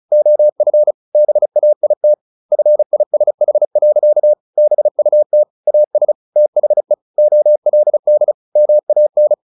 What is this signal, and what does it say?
Morse code, heard as a keyed tone, about 34 wpm, 600 Hz. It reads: OF BAIT FISH. BUT AS THE OLD MAN
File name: the old man and the sea / the old man and the sea 1 35wpm